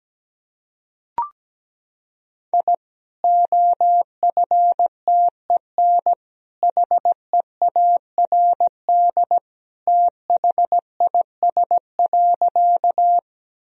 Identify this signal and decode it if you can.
Morse code, heard as a keyed tone, about 17 wpm, 700 Hz. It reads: I OFTEN HEARD THIS.
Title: Sentences from Top 1000 Words - Review Part 1 - 17wpm